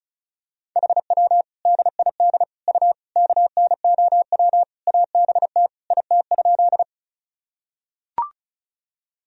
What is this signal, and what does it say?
Morse code, heard as a keyed tone, about 34 wpm, 700 Hz. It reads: HW DID U KNOW ABT IT?
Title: Sentences from Top 100 Words and Abbreviations 35wpm